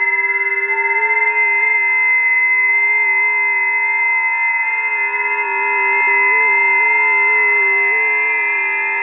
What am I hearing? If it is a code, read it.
FT8.
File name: tests / 20m busy / test 24